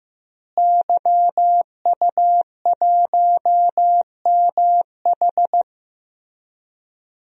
Morse code, heard as a keyed tone, about 16 wpm, 700 Hz. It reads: YU1MH
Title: DXCC-100 Call Signs - 5 Characters - Repeated 3x - 15wpm